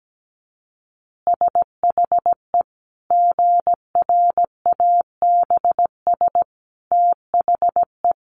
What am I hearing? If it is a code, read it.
Morse code, heard as a keyed tone, about 17 wpm, 700 Hz. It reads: SHE GRABS THE
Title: Easy Phrases 17wpm